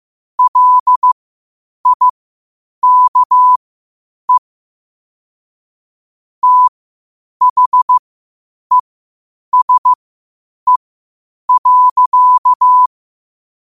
Morse code, heard as a keyed tone, about 15 wpm, spaced as for 8 wpm, 1000 Hz. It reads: LIKE THESE.
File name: qotdfiles / 8wpm